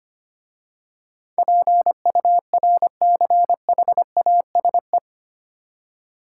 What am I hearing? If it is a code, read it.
Morse code, heard as a keyed tone, about 25 wpm, 700 Hz. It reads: PURCHASE
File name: 8 Letter Words - First 3 Letters Given - 25wpm